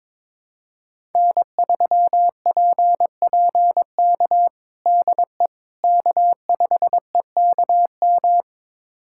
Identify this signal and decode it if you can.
Morse code, heard as a keyed tone, about 22 wpm, 700 Hz. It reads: N3PPK DE K5EKM
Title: USA Callsign DE Callsign - 5 Characters - 22wpm